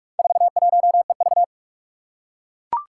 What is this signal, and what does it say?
Morse code, heard as a keyed tone, about 41 wpm, 700 Hz. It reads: 41EV
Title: Four Challenging Letter-Number Combinations 45wpm